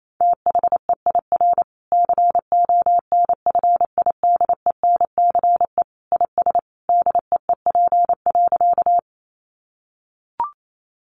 Morse code, heard as a keyed tone, about 28 wpm, 700 Hz. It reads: THEIR CONFIDENCE IS DEEP.
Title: Sentences from Top 1500 Words 28wpm